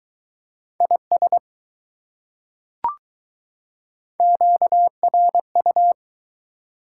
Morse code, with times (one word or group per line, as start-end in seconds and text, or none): 0.8-1.4 IS
4.2-5.9 QRU